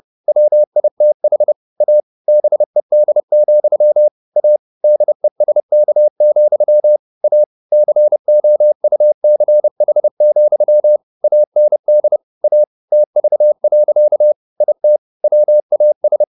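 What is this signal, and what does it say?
Morse code, heard as a keyed tone, about 30 wpm, 600 Hz. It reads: WITH A BED, A DESK, A COUCH, AND A TV. IT WAS
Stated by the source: Morse code at 30 wpm, the novel Never by Ken Follett in English, from CW med Gnister.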